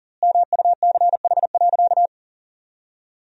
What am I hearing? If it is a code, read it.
Morse code, heard as a keyed tone, about 40 wpm, 700 Hz. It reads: MUCH.
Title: 4x Word-Spacing - Sentences from Top 300 Words - 40wpm